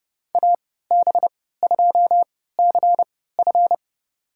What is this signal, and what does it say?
Morse code, heard as a keyed tone, about 30 wpm, spaced as for 17 wpm, 700 Hz. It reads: AB2CF
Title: USA Call Signs - 5 Characters - 3x Character Spacing - 30wpm